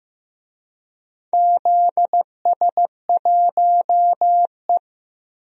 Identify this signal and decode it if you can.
Morse code, heard as a keyed tone, about 15 wpm, 700 Hz. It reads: ZS1E
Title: DXCC-100 Call Signs - 4 Characters - Repeated 3x - 15wpm